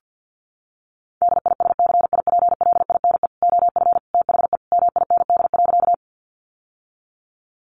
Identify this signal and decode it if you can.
Morse code, heard as a keyed tone, about 50 wpm, 700 Hz. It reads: DISCIPLINE OF THE MIND.
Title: Sentences from Top 900 Words 50wpm